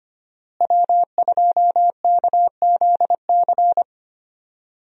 Morse code, heard as a keyed tone, about 25 wpm, 700 Hz. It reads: W2KZC
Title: USA Call Signs - 5 Characters - Repeated 3x - 25wpm